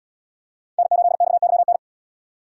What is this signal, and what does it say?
Morse code, heard as a keyed tone, about 70 wpm, 700 Hz. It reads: N8XYN